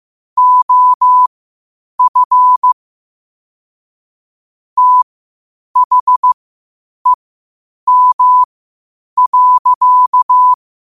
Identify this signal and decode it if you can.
Morse code, heard as a keyed tone, about 15 wpm, spaced as for 8 wpm, 1000 Hz. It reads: OF THEM.
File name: qotdfiles / 8wpm